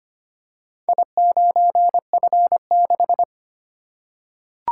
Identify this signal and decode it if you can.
Morse code, heard as a keyed tone, about 25 wpm, 700 Hz. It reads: I9F6
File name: practice / 25wpm / Four Letter-Number Combinations - 25wpm